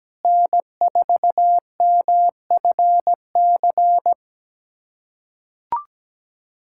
Morse code, heard as a keyed tone, about 17 wpm, 700 Hz. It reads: N4MFC
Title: USA Callsign DE Callsign 17wpm